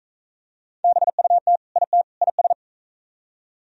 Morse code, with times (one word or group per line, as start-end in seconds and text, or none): 0.8-1.6 BUT
1.8-2.0 IT
2.2-2.5 IS